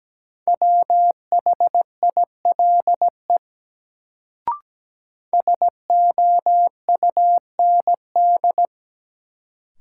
Morse code, heard as a keyed tone, about 17 wpm, 700 Hz. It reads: WHILE SOUND
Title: Sets of 3 Words - New Word Drill - 17wpm